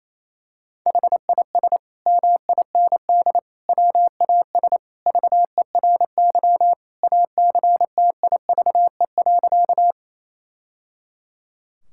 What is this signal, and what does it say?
Morse code, heard as a keyed tone, about 28 wpm, 700 Hz. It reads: HIS MIND WAS VERY ACTIVE.
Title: Sentences from Top 1000 Words - Review Part 6 - 28wpm